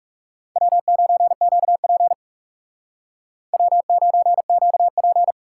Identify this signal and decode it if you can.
Morse code, heard as a keyed tone, about 38 wpm, 700 Hz. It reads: W9QP W9QP